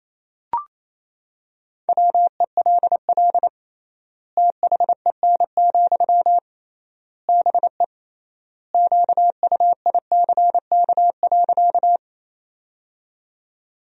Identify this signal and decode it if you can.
Morse code, heard as a keyed tone, about 27 wpm, 700 Hz. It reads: WELL THEN, BE QUICK.
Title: 3x Word-Spacing - Sentences from Top 500 Words - 28wpm